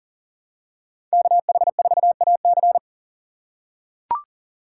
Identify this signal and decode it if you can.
Morse code, heard as a keyed tone, about 40 wpm, 700 Hz. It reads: KH4AC
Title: USA Call Signs - 5 Characters - 2x Character Spacing - 40wpm